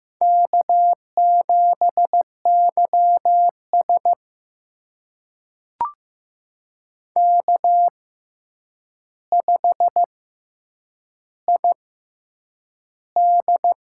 Morse code, heard as a keyed tone, about 15 wpm, spaced as for 8 wpm, 700 Hz. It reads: K7YS K5ID
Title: USA Call Signs - 4 Characters - 6x Character Spacing - 15wpm